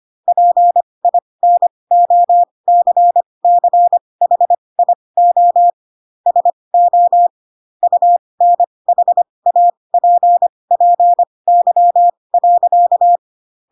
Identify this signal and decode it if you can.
Morse code, heard as a keyed tone, about 25 wpm, spaced as for 20 wpm, 700 Hz. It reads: PINOCCHIO SO UNHAPPY.
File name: Pinocchio / Pinocchio ch8 15 25 ik1hge